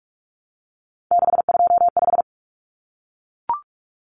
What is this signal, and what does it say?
Morse code, heard as a keyed tone, about 44 wpm, 700 Hz. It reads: 625
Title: Numbers - 3-Digit 45wpm